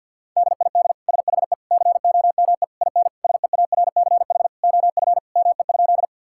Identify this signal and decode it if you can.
Morse code, heard as a keyed tone, about 50 wpm, 700 Hz. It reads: DID SHE COME IN SEARCH OF ME?
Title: Sentences from Top 1000 Words - Review Part 5 - 50wpm